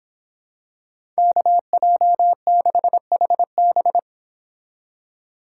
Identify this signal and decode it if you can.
Morse code, heard as a keyed tone, about 26 wpm, 700 Hz. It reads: KJ6HB